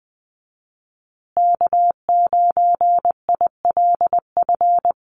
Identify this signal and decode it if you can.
Morse code, heard as a keyed tone, about 20 wpm, 700 Hz. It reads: K9ILF
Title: USA Call Signs Repeated 3x 20wpm